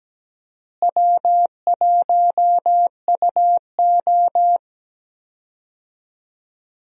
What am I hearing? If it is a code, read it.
Morse code, heard as a keyed tone, about 17 wpm, 700 Hz. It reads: W1UO
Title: USA Call Signs - 4 Characters - Repeated 2x - 17wpm